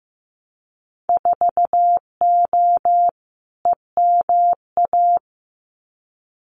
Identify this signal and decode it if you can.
Morse code, heard as a keyed tone, about 15 wpm, 700 Hz. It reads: 4O EMA
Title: Winter Field Day Exchanges 15wpm